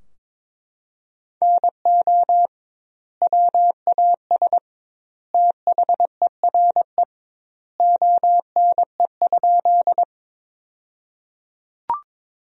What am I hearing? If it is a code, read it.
Morse code, heard as a keyed tone, about 21 wpm, 700 Hz. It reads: NO WAS THERE ONE?
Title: 2x Word-Spacing - Sentences from Top 100 Words - 22wpm